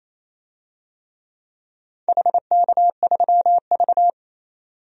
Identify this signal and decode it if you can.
Morse code, heard as a keyed tone, about 28 wpm, 700 Hz. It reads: HK3V
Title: DXCC-100 Call Signs - 4 Characters - 28wpm